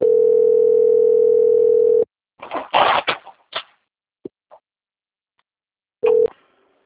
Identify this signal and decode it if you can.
Ringback tone (North America), 440 + 480 Hz.
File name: out- - - - - .5491